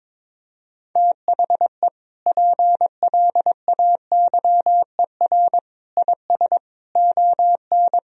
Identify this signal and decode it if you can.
Morse code, heard as a keyed tone, about 22 wpm, 700 Hz. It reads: THE PLAYER IS ON
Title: Sentences from Top 1000 Words - Review Part 4 - 22wpm